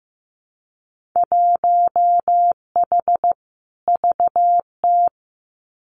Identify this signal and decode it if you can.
Morse code, heard as a keyed tone, about 15 wpm, 700 Hz. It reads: 1H VT
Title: Winter Field Day Exchanges 15wpm